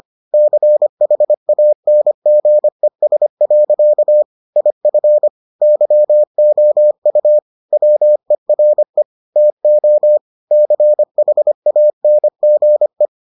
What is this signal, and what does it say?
Morse code, heard as a keyed tone, about 25 wpm, 600 Hz. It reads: CHANGES. IF YOU WERE TO CHANGE